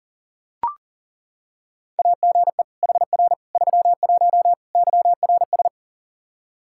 Morse code, heard as a keyed tone, about 40 wpm, 700 Hz. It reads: AGE HR 31 YRS
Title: Practice QSO - Exchange 3 - More Info - 40wpm